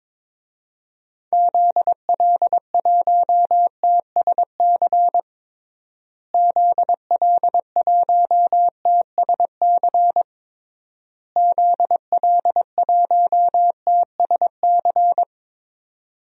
Morse code, heard as a keyed tone, about 21 wpm, 700 Hz. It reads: ZL1TSC ZL1TSC ZL1TSC
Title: DXCC-100 Call Signs - 6 Characters - Repeated 3x - 22wpm